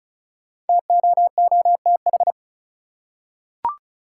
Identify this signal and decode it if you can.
Morse code, heard as a keyed tone, about 34 wpm, 700 Hz. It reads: TOOTH
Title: Five Letter Words - 35wpm